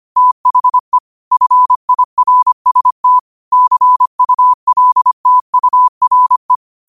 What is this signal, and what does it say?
Morse code, heard as a keyed tone, about 25 wpm, 1000 Hz. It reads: THE FIRST CULTURE
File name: qotdfiles / 25wpm